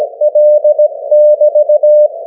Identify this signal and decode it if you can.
Morse code, heard as a keyed tone, about 17 wpm, 600 Hz. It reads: L<BT>
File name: GW3UEP / 16Dec12 9AM G3XIZ 501kHz